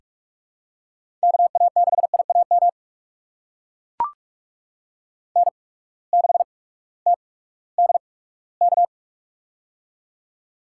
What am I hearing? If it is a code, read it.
Morse code, heard as a keyed tone, about 44 wpm, spaced as for 20 wpm, 690 Hz. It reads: KA6IAM N6TDK
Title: USA Call Signs - 8x Character Spacing - 45wpm